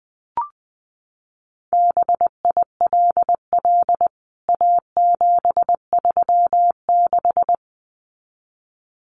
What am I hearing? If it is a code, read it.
Morse code, heard as a keyed tone, about 20 wpm, 700 Hz. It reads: BILL A736
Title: CWT Exchange 20wpm